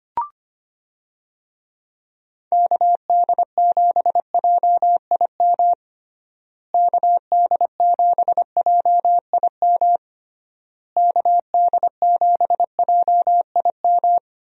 Morse code, heard as a keyed tone, about 25 wpm, 700 Hz. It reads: KD7JIM KD7JIM KD7JIM
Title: USA Call Signs - 6 Characters - Repeated 3x - 25wpm